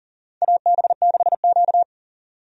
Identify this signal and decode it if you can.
Morse code, heard as a keyed tone, about 40 wpm, 700 Hz. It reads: AB6Q
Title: USA Call Signs - 4 Characters - 40wpm